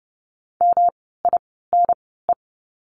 Morse code, heard as a keyed tone, about 30 wpm, spaced as for 17 wpm, 700 Hz. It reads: MINE
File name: course / 30:10wpm / 108 Lesson 18f - Letter m - Words 30wpm